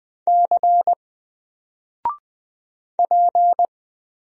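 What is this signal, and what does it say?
Morse code, heard as a keyed tone, about 20 wpm, 700 Hz. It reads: C P
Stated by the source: Single letter and number Morse drill at 20 wpm, from N4LCD.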